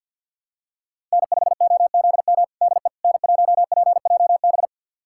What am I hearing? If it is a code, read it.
Morse code, heard as a keyed tone, about 46 wpm, 700 Hz. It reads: N5OZM DE N1PJB
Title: USA Callsign DE Callsign - 5 Characters - 50wpm